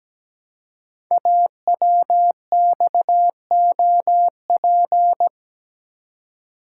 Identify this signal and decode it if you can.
Morse code, heard as a keyed tone, about 17 wpm, 700 Hz. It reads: AWXOP
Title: Code Groups - 5 Letters - 17wpm